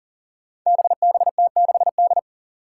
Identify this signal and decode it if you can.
Morse code, heard as a keyed tone, about 40 wpm, 700 Hz. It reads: BBT6D
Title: Five Challenging Letter-Number Combinations 40wpm